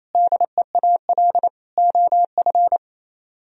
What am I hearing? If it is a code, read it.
Morse code, heard as a keyed tone, about 37 wpm, 700 Hz. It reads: DEAL OF